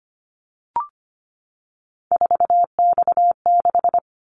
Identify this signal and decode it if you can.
Morse code, heard as a keyed tone, about 25 wpm, 700 Hz. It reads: 4X6
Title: Callsign Prefixes for Top 50 DXCC - 25wpm